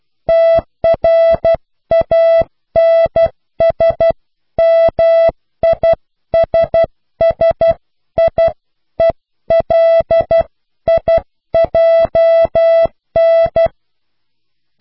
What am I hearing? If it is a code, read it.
Morse code, heard as a keyed tone, about 12 wpm, 660 Hz. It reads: TRANSMISSIELIJN